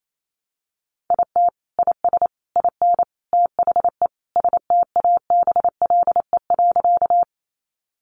Morse code, heard as a keyed tone, about 28 wpm, 700 Hz. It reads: IT IS IN THE STABLE.
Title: Sentences from Top 1500 Words 28wpm